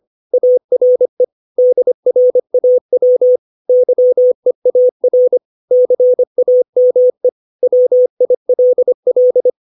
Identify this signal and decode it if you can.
Morse code, heard as a keyed tone, about 25 wpm, 500 Hz. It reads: ARE DRAW YEAR CAME WILL
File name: 632 words / 632 Words 25WPM